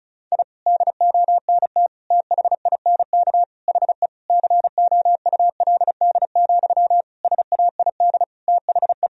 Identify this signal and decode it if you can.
Morse code, heard as a keyed tone, about 36 wpm, 700 Hz. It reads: I DONT THINK HE COULD, SAID THE
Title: Sentences from Top 1000 Words - Review Part 4 - 35wpm